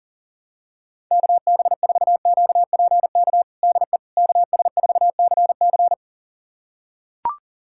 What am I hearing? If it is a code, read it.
Morse code, heard as a keyed tone, about 40 wpm, 700 Hz. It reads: KB4QPK DE KS4CC
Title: USA Callsign DE Callsign 40wpm